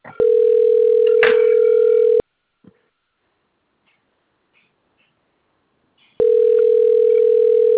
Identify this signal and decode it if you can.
Ringback tone (North America), 440 + 480 Hz.
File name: rg-1600-unknown- - - .57891